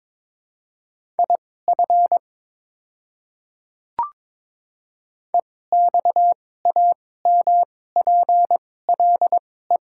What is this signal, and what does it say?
Morse code, heard as a keyed tone, about 22 wpm, spaced as for 16 wpm, 690 Hz. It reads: IF EXAMPLE
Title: 2x Character Spacing - Top 500 Words - 22wpm